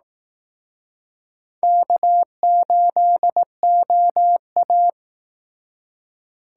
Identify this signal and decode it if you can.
Morse code, heard as a keyed tone, about 18 wpm, 700 Hz. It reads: K8OA
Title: Speed Racing - USA 1x2 Call Signs 15wpm